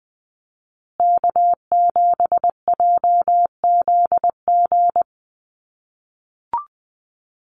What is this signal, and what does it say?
Morse code, heard as a keyed tone, about 20 wpm, 700 Hz. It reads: K7JZG
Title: USA Call Signs Repeated 3x 20wpm